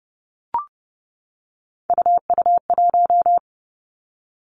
Morse code, heard as a keyed tone, about 30 wpm, 700 Hz. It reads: UU1
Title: Three Challenging Letter-Number Combinations 30wpm